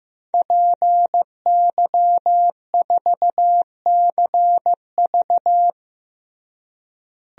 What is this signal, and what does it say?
Morse code, heard as a keyed tone, about 15 wpm, 700 Hz. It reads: PY4CV
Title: DXCC-100 Call Signs - 5 Characters - Repeated 2x - 15wpm